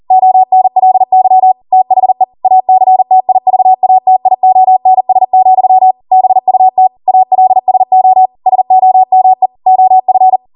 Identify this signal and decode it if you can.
Morse code, heard as a keyed tone, about 40 wpm, 750 Hz. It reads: ONLY THE ACTIVATIONS, BUT ALSO SOME OF